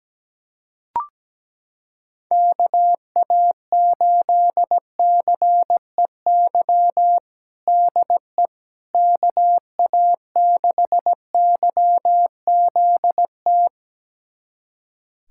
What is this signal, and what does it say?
Morse code, heard as a keyed tone, about 17 wpm, 700 Hz. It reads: KA8CEY DE KA6YZT
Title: USA Callsign DE Callsign - 6 Characters - 17wpm